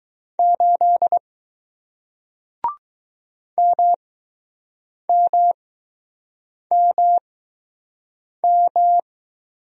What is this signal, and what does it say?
Morse code, heard as a keyed tone, about 20 wpm, spaced as for 6 wpm, 700 Hz. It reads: 8 MMMM